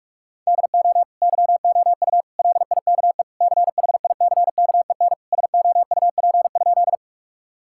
Morse code, heard as a keyed tone, about 44 wpm, 700 Hz. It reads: DO YOU LIKE CHICKEN SOUP?
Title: Sentences from Top 1000 Words - Review Part 6 - 45wpm